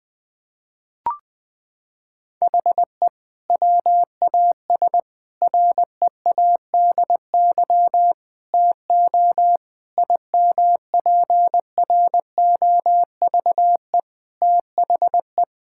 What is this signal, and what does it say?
Morse code, heard as a keyed tone, about 20 wpm, 700 Hz. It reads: HE WAS READY TO IMPROVE THE
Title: Sentences from Top 1000 Words - Review Part 5 - 20wpm